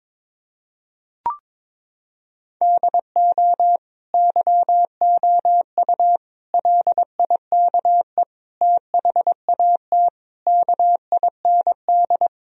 Morse code, heard as a keyed tone, about 22 wpm, 700 Hz. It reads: DO YOU LIKE THAT KIND
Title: Sentences from Top 1000 Words - Review Part 4 - 22wpm